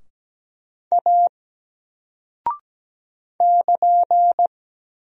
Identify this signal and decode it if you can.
Morse code, heard as a keyed tone, about 17 wpm, 700 Hz. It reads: A <KN>
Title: Single Character (Letter:Number:Punctuation:Prosign) 17wpm